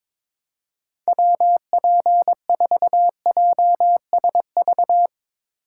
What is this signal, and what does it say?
Morse code, heard as a keyed tone, about 22 wpm, 700 Hz. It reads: WP4JSV
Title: USA Call Signs - 6 Characters - 2x Character Spacing - 22wpm